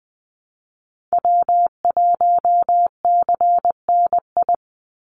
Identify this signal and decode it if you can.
Morse code, heard as a keyed tone, about 19 wpm, 700 Hz. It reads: W1CNI
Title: USA Call Signs 20wpm